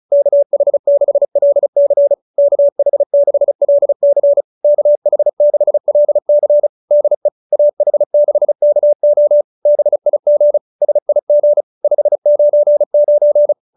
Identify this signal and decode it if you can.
Morse code, heard as a keyed tone, about 34 wpm, 590 Hz. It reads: KH6LC KH6LC KH6LC DE AH6KO BIG SIG 599